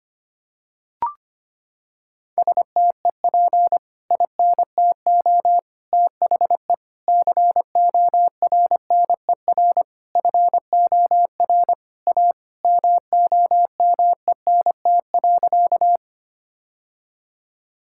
Morse code, heard as a keyed tone, about 25 wpm, 700 Hz. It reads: STEP INTO THE CORNER FOR A MOMENT.
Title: Sentences from Top 1600 Words 25wpm